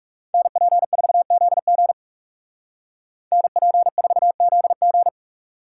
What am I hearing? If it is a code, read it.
Morse code, heard as a keyed tone, about 36 wpm, 700 Hz. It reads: NP4ZG NP4ZG
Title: Speed Racing - USA 2x2 Call Signs 30wpm